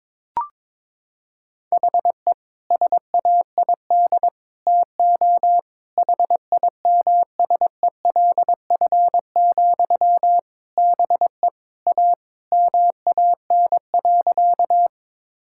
Morse code, heard as a keyed tone, about 22 wpm, 700 Hz. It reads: HE SAID TO HIMSELF, BE A MAN.
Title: Sentences from Top 1600 Words 22wpm